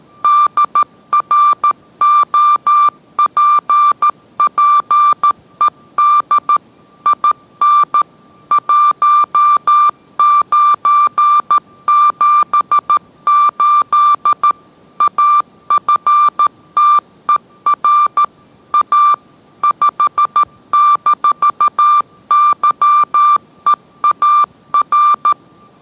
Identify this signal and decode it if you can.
Morse code, heard as a keyed tone, about 13 wpm, 1180 Hz. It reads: DROPPED IN 1978 AFTER A 5-YEAR